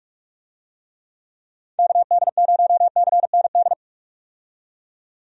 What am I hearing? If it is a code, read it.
Morse code, heard as a keyed tone, about 46 wpm, 700 Hz. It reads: KD0CND
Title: USA Call Signs - 6 Characters - 45wpm